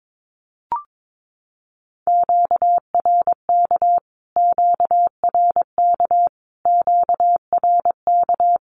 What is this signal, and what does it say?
Morse code, heard as a keyed tone, about 22 wpm, 700 Hz. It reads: QRK QRK QRK
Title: Q-Codes Repeated 3x 22wpm